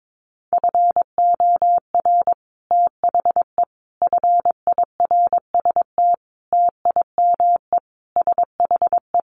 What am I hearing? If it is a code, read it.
Morse code, heard as a keyed tone, about 22 wpm, 700 Hz. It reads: FOR THE FIRST TIME SHE
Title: Sentences from Top 1300 Words 22wpm